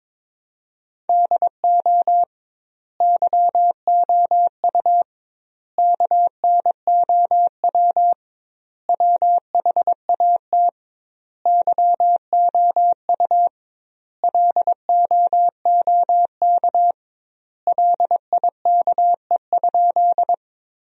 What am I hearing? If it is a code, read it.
Morse code, heard as a keyed tone, about 22 wpm, 700 Hz. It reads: DO YOU KNOW WHAT YOU LOOK LIKE?
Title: 2x Word-Spacing - Sentences from Top 100 Words - 22wpm